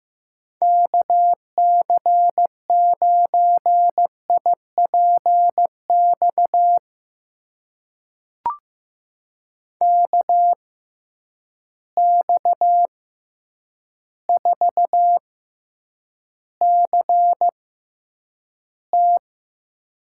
Morse code, heard as a keyed tone, about 15 wpm, spaced as for 8 wpm, 700 Hz. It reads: KC9IPX KX4CT
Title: USA Call Signs - 6x Character Spacing - 15wpm